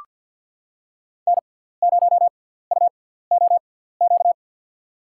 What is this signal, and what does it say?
Morse code, heard as a keyed tone, about 53 wpm, spaced as for 17 wpm, 700 Hz. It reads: N0UOQ
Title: USA Call Signs - 6x Character Spacing - 50wpm